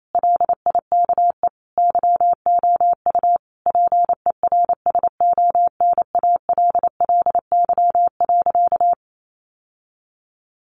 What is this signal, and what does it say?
Morse code, heard as a keyed tone, about 28 wpm, 700 Hz. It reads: LIKE YOU PERSONALLY.
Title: Sentences from Top 1400 Words 28wpm